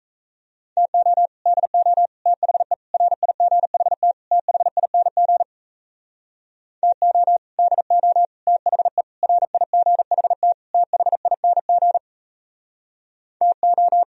Morse code, heard as a keyed tone, about 38 wpm, 700 Hz. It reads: TO DO THE RIGHT THING TO DO THE RIGHT THING TO